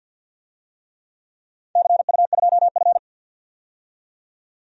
Morse code, heard as a keyed tone, about 47 wpm, 700 Hz. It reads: CU2F
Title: DXCC-100 Call Signs - 4 Characters - 50wpm